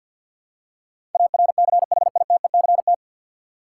Morse code, heard as a keyed tone, about 45 wpm, 700 Hz. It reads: ARCHITECT
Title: Practice QSO - Exchange 3 - More Info - 50wpm